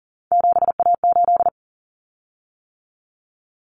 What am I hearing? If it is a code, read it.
Morse code, heard as a keyed tone, about 40 wpm, 700 Hz. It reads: 7A8
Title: Common Cut Numbers - 3-Digit 40wpm